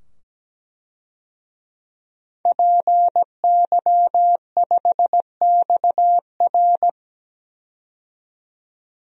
Morse code, heard as a keyed tone, about 17 wpm, 700 Hz. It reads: PY5XR
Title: DXCC-100 Call Signs - 5 Characters - 17wpm